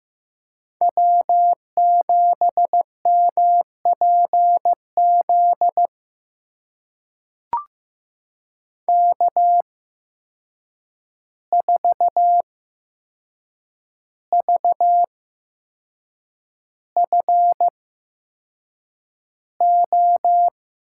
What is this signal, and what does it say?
Morse code, heard as a keyed tone, about 15 wpm, spaced as for 6 wpm, 700 Hz. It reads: W7MPZ K4VFO
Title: USA Call Signs - 5 Characters - 8x Character Spacing - 15wpm